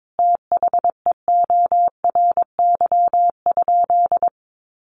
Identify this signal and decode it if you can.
Morse code, heard as a keyed tone, about 22 wpm, 700 Hz. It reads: THEORY?
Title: Sentences from Top 1000 Words 22wpm